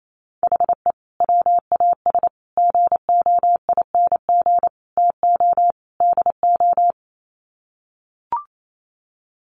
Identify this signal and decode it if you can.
Morse code, heard as a keyed tone, about 28 wpm, 700 Hz. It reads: HE WAS GOING TO DO
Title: Sets of 5 Words Encore2 28wpm